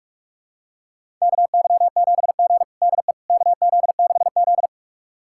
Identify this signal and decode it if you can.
Morse code, heard as a keyed tone, about 43 wpm, 700 Hz. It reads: KY7G DE KZ6Z
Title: USA Callsign DE Callsign - 4 Characters - 45wpm